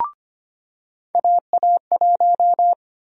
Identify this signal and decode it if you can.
Morse code, heard as a keyed tone, about 25 wpm, 700 Hz. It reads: AA1
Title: USA Call Sign Prefixes - 25wpm